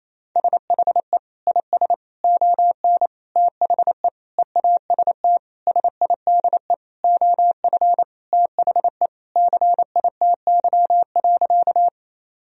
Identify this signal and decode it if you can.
Morse code, heard as a keyed tone, about 28 wpm, 700 Hz. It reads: SHE IS ON THE EAST SIDE OF THE CITY.
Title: Sentences from Top 1000 Words - Review Part 3 - 28wpm